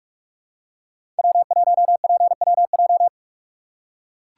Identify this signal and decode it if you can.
Morse code, heard as a keyed tone, about 44 wpm, 700 Hz. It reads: W1PWJ
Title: USA Call Signs - 5 Characters - 6x Character Spacing - 45wpm